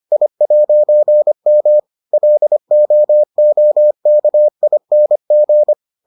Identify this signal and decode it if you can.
Morse code, heard as a keyed tone, about 25 wpm, 600 Hz. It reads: I'M LOOKING